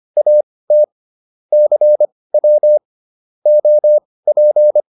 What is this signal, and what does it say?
Morse code, heard as a keyed tone, about 25 wpm, spaced as for 18 wpm, 600 Hz. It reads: AT CW OP
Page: intermediate-practice-files